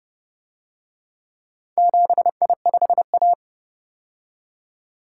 Morse code, heard as a keyed tone, about 32 wpm, 700 Hz. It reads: 7I5A